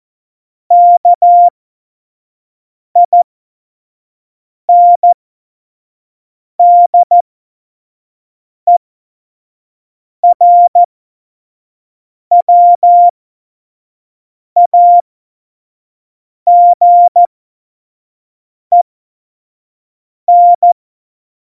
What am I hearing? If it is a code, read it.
Morse code, heard as a keyed tone, about 14 wpm, spaced as for 5 wpm, 700 Hz. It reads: KINDERWAGEN
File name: Morsekurs / Witze / Witz 08. Mai 2018, 25 F70